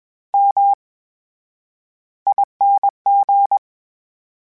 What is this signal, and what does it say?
Morse code, heard as a keyed tone, about 21 wpm, spaced as for 11 wpm, 800 Hz. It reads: MING
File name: Special INGs-mp3s / Special INGs-mp3s / VerySlowToFaster Ing @20